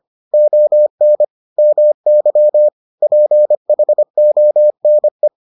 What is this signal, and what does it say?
Morse code, heard as a keyed tone, about 25 wpm, 600 Hz. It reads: ON MY PHONE